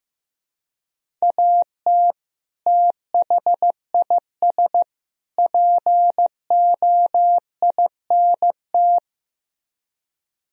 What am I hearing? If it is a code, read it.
Morse code, heard as a keyed tone, about 15 wpm, 700 Hz. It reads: AT THIS POINT
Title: Speed-Racing - Sets of 3 Words 15wpm